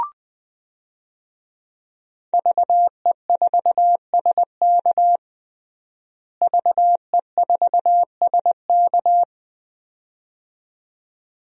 Morse code, heard as a keyed tone, about 20 wpm, 700 Hz. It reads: VE4SK VE4SK
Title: DXCC-100 Call Signs - 5 Characters - Repeated 2x - 20wpm